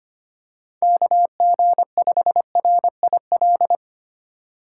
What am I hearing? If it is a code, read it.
Morse code, heard as a keyed tone, about 25 wpm, 700 Hz. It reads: KG5RIL
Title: USA Call Signs - 2x Character Spacing - 25wpm